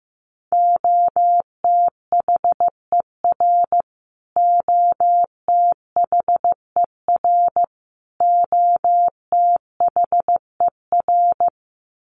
Morse code, heard as a keyed tone, about 14 wpm, 700 Hz. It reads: OTHER OTHER OTHER
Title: Top 300 Words Repeated 15wpm